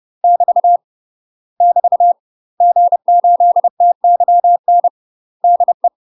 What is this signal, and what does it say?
Morse code, heard as a keyed tone, about 30 wpm, 700 Hz. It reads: <BT> <BT> G8TYN DE
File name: UK MASTER QSOs TRAINING / 30WPM / UK MASTER QSOs 0008